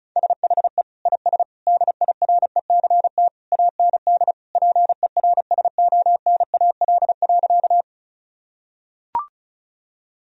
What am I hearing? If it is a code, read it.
Morse code, heard as a keyed tone, about 36 wpm, 700 Hz. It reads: SHE IS DIRECT AND PERSONAL.
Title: Sentences from Top 1000 Words - Review Part 4 - 35wpm